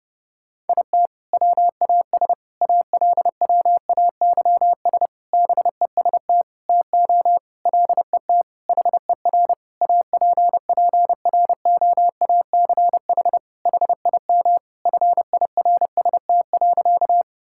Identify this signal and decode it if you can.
Morse code, heard as a keyed tone, about 30 wpm, 700 Hz. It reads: IT WAS ALWAYS BEST TO LET HER APPROACH HIM FIRST.
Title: Sentences from Top 1000 Words - Review Part 6 - 30wpm